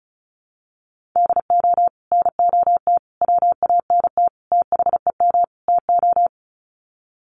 Morse code, heard as a keyed tone, about 34 wpm, 700 Hz. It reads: DO NOT WANT THEM TO
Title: Sets of 5 Words Encore 35wpm